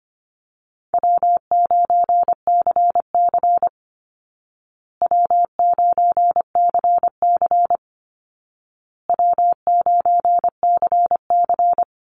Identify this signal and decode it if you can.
Morse code, heard as a keyed tone, about 25 wpm, 700 Hz. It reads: W9CC W9CC W9CC
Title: Easy Call Signs Repeated 3x - 25wpm